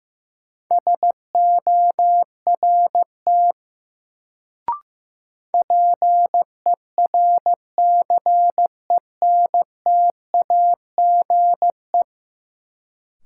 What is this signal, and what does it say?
Morse code, heard as a keyed tone, about 15 wpm, 700 Hz. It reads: SORT PERCENTAGE
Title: Sentences from Top 1300 Words - New Word Drill Alt - 15wpm